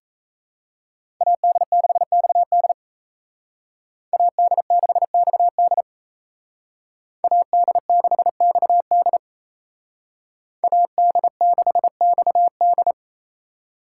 Morse code, heard as a keyed tone, about 33 wpm, 700 Hz. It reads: AD6XD AD6XD AD6XD AD6XD
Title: Speed Racing - USA 2x2 Call Signs 28wpm